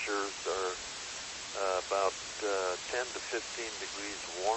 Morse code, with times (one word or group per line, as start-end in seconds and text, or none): none